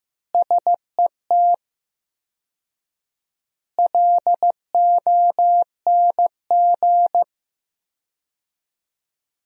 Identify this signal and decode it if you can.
Morse code, heard as a keyed tone, about 15 wpm, 700 Hz. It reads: SET LONG
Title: 4x Word-Spacing - Sentences from Top 300 Words - 15wpm